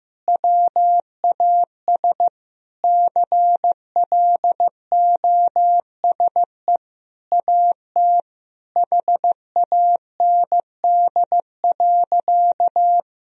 Morse code, heard as a keyed tone, about 16 wpm, 700 Hz. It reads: WAS CLOSE AT HAND.
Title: Sentences from Top 1000 Words - Review Part 1 - 15wpm